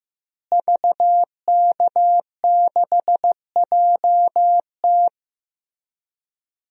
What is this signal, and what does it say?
Morse code, heard as a keyed tone, about 15 wpm, 700 Hz. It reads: VK6JT
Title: DXCC-100 Call Signs - 5 Characters - Repeated 2x - 15wpm